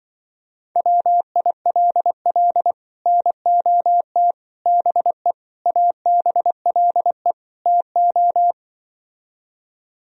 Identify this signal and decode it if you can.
Morse code, heard as a keyed tone, about 24 wpm, 700 Hz. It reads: WILL NOT BE ABLE TO
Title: Speed-Racing - Sets of 5 Words 20wpm